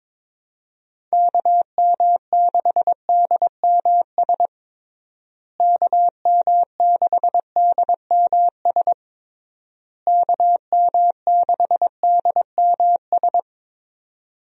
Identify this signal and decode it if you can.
Morse code, heard as a keyed tone, about 22 wpm, 700 Hz. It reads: KM6DMS KM6DMS KM6DMS
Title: USA Call Signs - 6 Characters - Repeated 3x - 22wpm